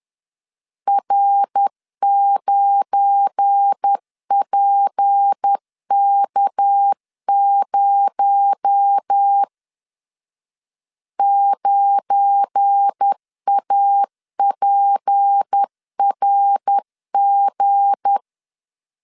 Morse code, heard as a keyed tone, about 10 wpm, 800 Hz. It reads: R9PK0 9APRG
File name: LEZIONI / 7 GF9Z / 7 GF9 10wpm 2747